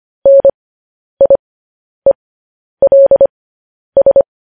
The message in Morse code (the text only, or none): NIELS